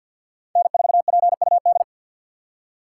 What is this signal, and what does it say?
Morse code, heard as a keyed tone, about 53 wpm, 700 Hz. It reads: N4PUD
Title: USA Call Signs - 2x Character Spacing - 50wpm